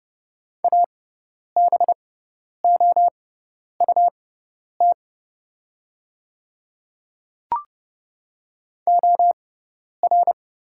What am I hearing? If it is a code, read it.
Morse code, heard as a keyed tone, about 30 wpm, spaced as for 10 wpm, 700 Hz. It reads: ABOUT OR
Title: 6x Character Spacing - Top 100 Words - 30wpm